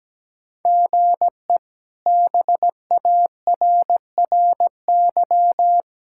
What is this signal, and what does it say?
Morse code, heard as a keyed tone, about 17 wpm, 700 Hz. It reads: GE BARRY
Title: K1USN SST QSO Practice - 17wpm